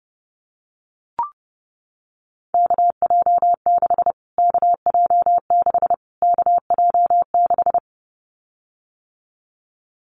Morse code, heard as a keyed tone, about 32 wpm, 700 Hz. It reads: KJ6 KJ6 KJ6
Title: Callsign Prefixes for Top 25 DXCC - Repeated 3x - 30wpm